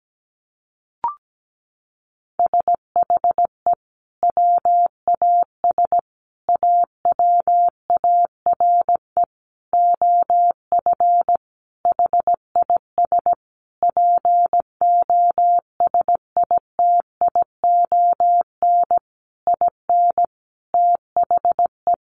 Morse code, heard as a keyed tone, about 17 wpm, 700 Hz. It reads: SHE WAS AWARE OF HIS POSITION IN THE
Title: Sentences from Top 1200 Words 17wpm